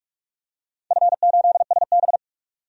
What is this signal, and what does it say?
Morse code, heard as a keyed tone, about 47 wpm, 700 Hz. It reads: F8SB
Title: DXCC-100 Call Signs - 4 Characters - 45wpm